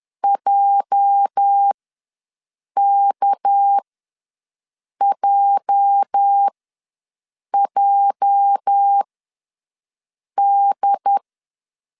Morse code, heard as a keyed tone, about 10 wpm, spaced as for 5 wpm, 800 Hz. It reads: JKJJD
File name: LEZIONI / 8 YJ45 / 8 YJ ws 10wpm 4043